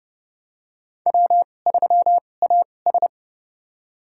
Morse code, heard as a keyed tone, about 31 wpm, spaced as for 22 wpm, 700 Hz. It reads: W3AS
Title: USA Call Signs - 2x Character Spacing - 30wpm